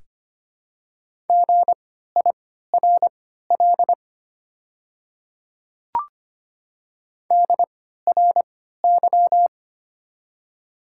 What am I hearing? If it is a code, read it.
Morse code, heard as a keyed tone, about 25 wpm, spaced as for 14 wpm, 700 Hz. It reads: GIRL DRY
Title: 3x Character Spacing - Top 500 Words - 25wpm